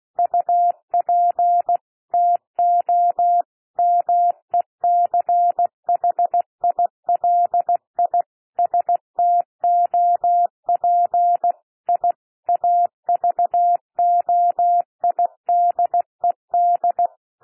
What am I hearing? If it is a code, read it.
Morse code, heard as a keyed tone, about 16 wpm, 700 Hz. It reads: UP TO MECHILI STOP I AVOIDED
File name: cwmp3 / P1 SOUND PASSAGE 16 wpm